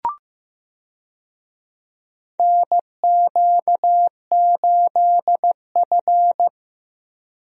Morse code, heard as a keyed tone, about 15 wpm, 700 Hz. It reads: NQ8F
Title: USA Call Signs - 4 Characters - Repeated 3x - 15wpm